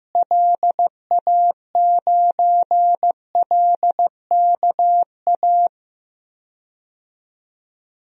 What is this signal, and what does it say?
Morse code, heard as a keyed tone, about 15 wpm, 700 Hz. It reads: LA9LKA
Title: DXCC-100 Call Signs - 6 Characters - Repeated 2x - 15wpm